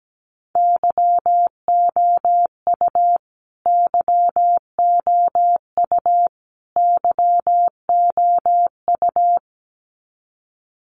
Morse code, heard as a keyed tone, about 17 wpm, 700 Hz. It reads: YOU YOU YOU
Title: Top 500 Words Repeated 17wpm